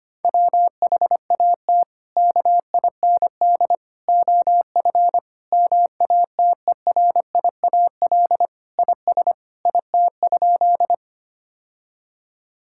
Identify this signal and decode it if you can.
Morse code, heard as a keyed tone, about 25 wpm, 700 Hz. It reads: WHAT KIND OF MATERIAL IS IT?
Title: Sentences from Top 1000 Words - Review Part 5 - 25wpm